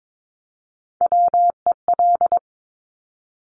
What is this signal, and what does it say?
Morse code, heard as a keyed tone, about 22 wpm, 700 Hz. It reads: WEL
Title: Top Three Letter Combinations in Words 22wpm